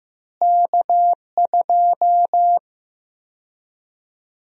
Morse code, heard as a keyed tone, about 15 wpm, 700 Hz. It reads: K2
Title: USA Call Sign Prefixes - Repeated 2x - 15wpm